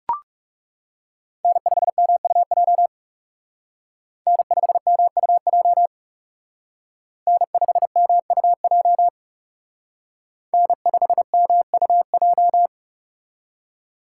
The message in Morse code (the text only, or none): N5MUJ N5MUJ N5MUJ N5MUJ